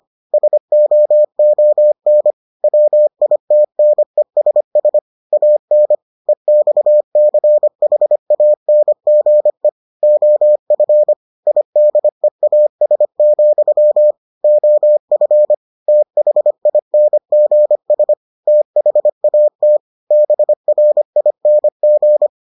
Morse code, heard as a keyed tone, about 25 wpm, 600 Hz. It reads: SOON WITNESS AN EXCHANGE OF IDEAS, OF THINGS THAT BRING